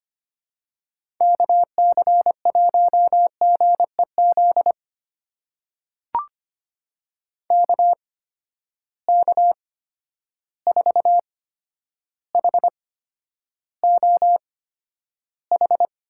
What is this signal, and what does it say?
Morse code, heard as a keyed tone, about 25 wpm, spaced as for 11 wpm, 700 Hz. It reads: KC1GEZ KK4HOH
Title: USA Call Signs - 6 Characters - 8x Character Spacing - 25wpm